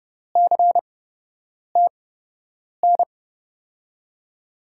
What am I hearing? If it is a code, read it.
Morse code, heard as a keyed tone, about 30 wpm, spaced as for 8 wpm, 700 Hz. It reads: CTN
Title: USA Call Signs - 6 Characters - 8x Character Spacing - 30wpm